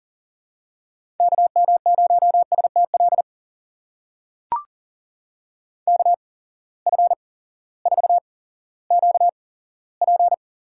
Morse code, heard as a keyed tone, about 40 wpm, spaced as for 18 wpm, 700 Hz. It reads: KM0STL KF4QP
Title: USA Call Signs - 6 Characters - 8x Character Spacing - 40wpm